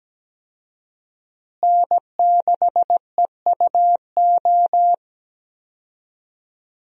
Morse code, heard as a keyed tone, about 17 wpm, 700 Hz. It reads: N6EUO
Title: USA Call Signs - 5 Characters - Repeated 3x - 17wpm